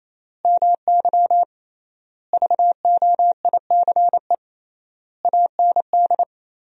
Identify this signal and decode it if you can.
Morse code, heard as a keyed tone, about 28 wpm, 700 Hz. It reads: MY VOICE AND
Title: 3x Word-Spacing - Sentences from Top 500 Words - 28wpm